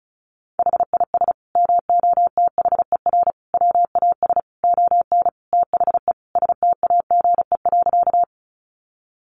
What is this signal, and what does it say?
Morse code, heard as a keyed tone, about 34 wpm, 700 Hz. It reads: HIS MOTHER WAS ON THE STAGE.
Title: Sentences from Top 1200 Words 35wpm